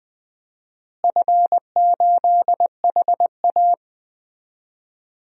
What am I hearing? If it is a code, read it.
Morse code, heard as a keyed tone, about 20 wpm, 700 Hz. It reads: F8HA
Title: DXCC-100 Call Signs - 4 Characters - Repeated 2x - 20wpm